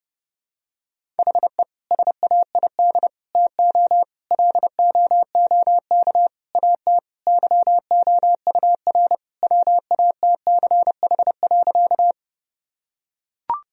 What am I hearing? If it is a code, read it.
Morse code, heard as a keyed tone, about 30 wpm, 700 Hz. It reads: HE SAID TO LOOK AT YOUR WATCH.
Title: Sentences from Top 1000 Words - Review Part 1 - 30wpm